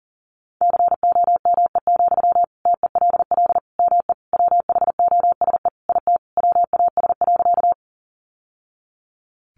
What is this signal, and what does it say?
Morse code, heard as a keyed tone, about 40 wpm, 700 Hz. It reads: COME, TELL ME WHOSE IT WAS.
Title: Sentences from Top 1200 Words 40wpm